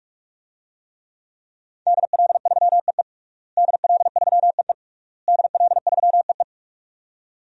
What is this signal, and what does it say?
Morse code, heard as a keyed tone, about 41 wpm, 700 Hz. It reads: DL3EE DL3EE DL3EE
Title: DXCC-100 Call Signs - 5 Characters - Repeated 3x - 45wpm